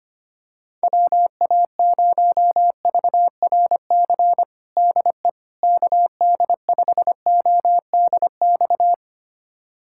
Morse code, heard as a keyed tone, about 25 wpm, 700 Hz. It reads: WA0VRC DE KD5ODX
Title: USA Callsign DE Callsign - 6 Characters - 25wpm